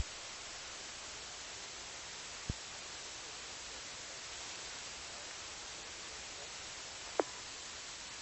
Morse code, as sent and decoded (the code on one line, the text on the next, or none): none